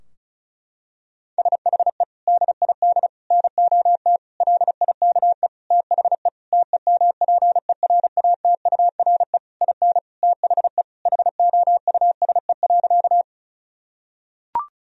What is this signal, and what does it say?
Morse code, heard as a keyed tone, about 34 wpm, 700 Hz. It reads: SHE DID NOT LIKE THE TEMPERATURE IN THE HOUSE.
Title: Sentences from Top 1000 Words - Review Part 4 - 35wpm